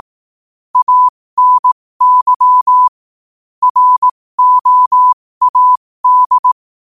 Morse code, heard as a keyed tone, about 18 wpm, spaced as for 15 wpm, 1000 Hz. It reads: ANY ROAD
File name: qotdfiles / 15wpm